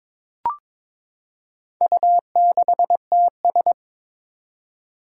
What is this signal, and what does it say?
Morse code, heard as a keyed tone, about 23 wpm, 700 Hz. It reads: U6TS
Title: Four Challenging Letter-Number Combinations 22wpm